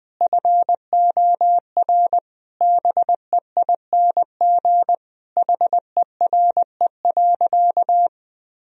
Morse code, heard as a keyed tone, about 20 wpm, 700 Hz. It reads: FOR BEING HERE.
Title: Sentences from Top 1000 Words - Review Part 2 - 20wpm